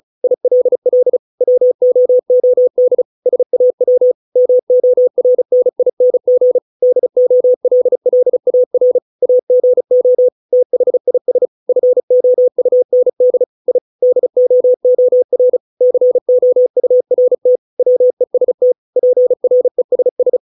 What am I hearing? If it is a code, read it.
Morse code, heard as a keyed tone, about 34 wpm, 500 Hz. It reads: ILL WOOD SAW MORNING DOLLAR AGO THIS FOUND I DOOR COURT WEST PRESS